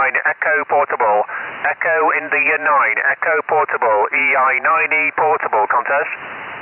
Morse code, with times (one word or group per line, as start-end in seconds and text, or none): none